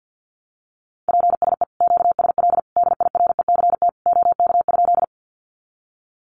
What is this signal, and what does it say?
Morse code, heard as a keyed tone, about 50 wpm, 700 Hz. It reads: PSE QSL DIRECT OK?
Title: Common QSO Phrases 50wpm